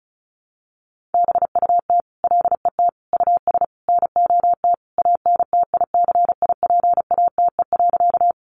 Morse code, heard as a keyed tone, about 34 wpm, 700 Hz. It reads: BUT LET US NOT ANTICIPATE.
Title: Sentences from Top 1500 Words 35wpm